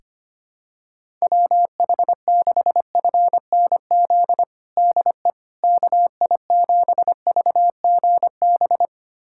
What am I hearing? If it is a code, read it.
Morse code, heard as a keyed tone, about 25 wpm, 700 Hz. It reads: WH6FNZ DE KI7VGB
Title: USA Callsign DE Callsign - 6 Characters - 25wpm